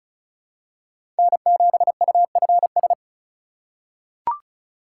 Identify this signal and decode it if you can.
Morse code, heard as a keyed tone, about 36 wpm, 700 Hz. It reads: N7UFS
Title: USA Call Signs - 5 Characters - 4x Character Spacing - 35wpm